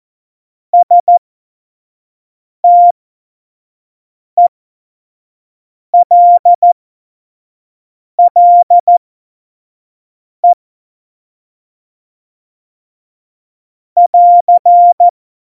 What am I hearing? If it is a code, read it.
Morse code, heard as a keyed tone, about 14 wpm, spaced as for 5 wpm, 700 Hz. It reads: STELLE <AR>
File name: Morsekurs / Witze / Witz 04. Juni 2019, 25 F70